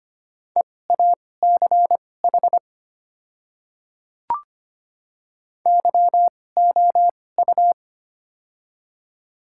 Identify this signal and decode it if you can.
Morse code, heard as a keyed tone, about 25 wpm, spaced as for 18 wpm, 700 Hz. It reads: EACH YOU
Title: 2x Character Spacing - Top 100 Words - 25wpm